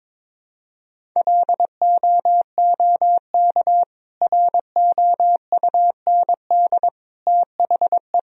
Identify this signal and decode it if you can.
Morse code, heard as a keyed tone, about 22 wpm, 700 Hz. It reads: LOOK ROUND THE
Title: Sentences from Top 1000 Words - Review Part 1 - 22wpm